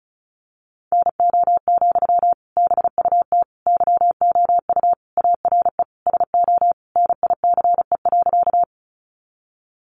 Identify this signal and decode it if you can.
Morse code, heard as a keyed tone, about 34 wpm, 700 Hz. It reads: NO, BUT YOU ARE SO NICE.
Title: Sentences from Top 1100 Words 35wpm